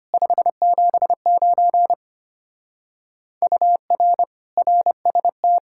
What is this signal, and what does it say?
Morse code, heard as a keyed tone, about 31 wpm, 700 Hz. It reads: 579 UR RST
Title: Speed-Racing Common QSO Phrases Encore - 25wpm